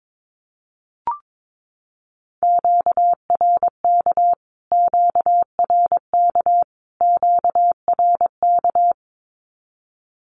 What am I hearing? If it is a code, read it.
Morse code, heard as a keyed tone, about 22 wpm, 700 Hz. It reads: QRK QRK QRK
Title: Q-Codes Repeated 3x 22wpm